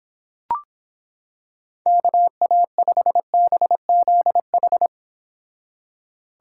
Morse code, heard as a keyed tone, about 26 wpm, 700 Hz. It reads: KA5BZH